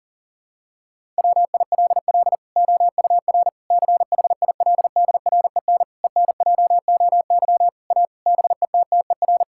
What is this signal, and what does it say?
Morse code, heard as a keyed tone, about 40 wpm, 700 Hz. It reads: WILL OUR CHILDREN ENJOY A BETTER
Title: Sentences from Top 1000 Words - Review Part 5 - 40wpm